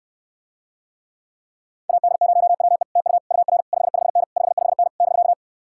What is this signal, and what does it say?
Morse code, heard as a keyed tone, about 66 wpm, 700 Hz. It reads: AD0WE TU UR 55N 55N <BK>